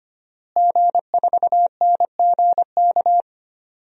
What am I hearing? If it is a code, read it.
Morse code, heard as a keyed tone, about 25 wpm, 700 Hz. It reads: G4NGK
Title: DXCC-100 Call Signs - 5 Characters - Repeated 3x - 25wpm